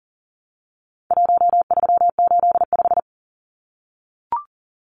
Morse code, heard as a keyed tone, about 40 wpm, 700 Hz. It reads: 1385
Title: Numbers - 4-Digit 40wpm